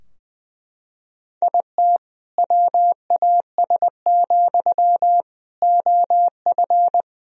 Morse code, heard as a keyed tone, about 21 wpm, 700 Hz. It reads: IT WAS, OF
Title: Sentences from Top 1000 Words - Review Part 3 - 20wpm